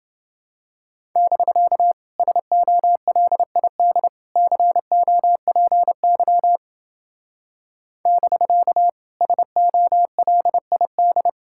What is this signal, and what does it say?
Morse code, heard as a keyed tone, about 29 wpm, 700 Hz. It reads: <BK> SOLID COPY <BK> SOLID